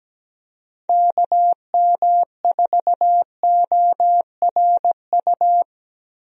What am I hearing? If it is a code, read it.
Morse code, heard as a keyed tone, about 17 wpm, 700 Hz. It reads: KM4ORU
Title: USA Call Signs - 6 Characters - Repeated 2x - 17wpm